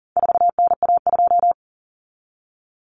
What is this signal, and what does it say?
Morse code, heard as a keyed tone, about 40 wpm, 700 Hz. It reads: 4NA2
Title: Common Cut Numbers - 4-Digit 40wpm